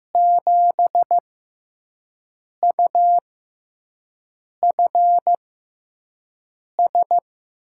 Morse code, heard as a keyed tone, about 15 wpm, spaced as for 5 wpm, 700 Hz. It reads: 7UFS
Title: USA Call Signs - 5 Characters - 6x Character Spacing - 15wpm